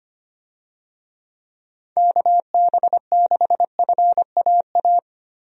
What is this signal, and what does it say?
Morse code, heard as a keyed tone, about 25 wpm, 700 Hz. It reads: KB6FAA
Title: USA Call Signs - 6 Characters - 25wpm